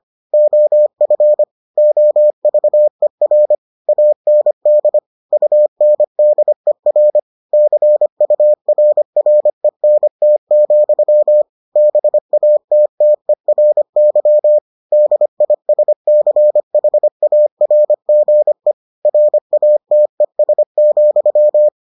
Morse code, heard as a keyed tone, about 25 wpm, 600 Hz. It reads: OF OVER AND UNDER CURRENT, BATTERY DISCHARGE RATES,